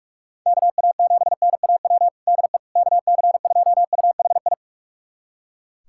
Morse code, heard as a keyed tone, about 42 wpm, 700 Hz. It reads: KA7NAW DE KC2UHI
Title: USA Callsign DE Callsign 45wpm